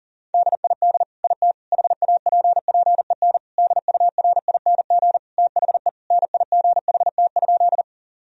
Morse code, heard as a keyed tone, about 40 wpm, 700 Hz. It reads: DID IT HAPPEN DURING THE NIGHT?
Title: Sentences from Top 1000 Words - Review Part 2 - 40wpm